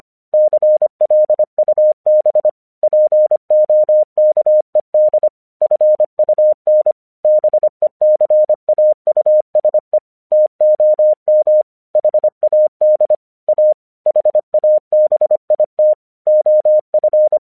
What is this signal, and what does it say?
Morse code, heard as a keyed tone, about 26 wpm, 610 Hz. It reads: CLUB POKED FUN BECAUSE TOM HAD A HABIT OF
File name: 1 English 25wpm / AQC11